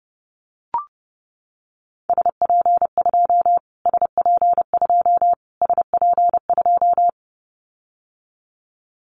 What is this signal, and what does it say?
Morse code, heard as a keyed tone, about 30 wpm, 700 Hz. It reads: SP2 SP2 SP2
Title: Callsign Prefixes for Top 25 DXCC - Repeated 3x - 30wpm